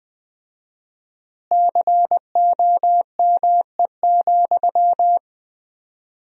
Morse code, heard as a keyed tone, about 20 wpm, 700 Hz. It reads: COME,
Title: 4x Word-Spacing - Sentences from Top 100 Words - 20wpm